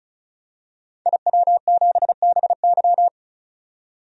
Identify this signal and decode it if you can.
Morse code, heard as a keyed tone, about 36 wpm, 700 Hz. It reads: IW7BY
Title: DXCC-100 Call Signs - 5 Characters - 35wpm